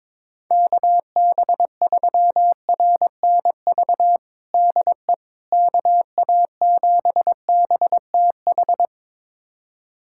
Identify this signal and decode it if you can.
Morse code, heard as a keyed tone, about 22 wpm, 700 Hz. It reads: KB3RNV DE KA7BTH
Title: USA Callsign DE Callsign 22wpm